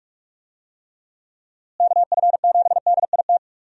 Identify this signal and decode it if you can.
Morse code, heard as a keyed tone, about 41 wpm, 700 Hz. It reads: KF7DIT
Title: USA Call Signs - 6 Characters - Repeated 3x - 45wpm